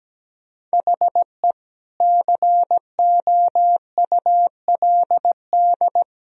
Morse code, heard as a keyed tone, about 17 wpm, 700 Hz. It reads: HE COULD
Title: Sentences from Top 1000 Words - Review Part 2 - 17wpm